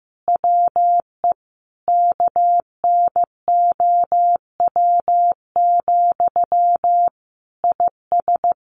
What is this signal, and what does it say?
Morse code, heard as a keyed tone, about 15 wpm, 700 Hz. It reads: WE KNOW, IS
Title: Sentences from Top 1400 Words 15wpm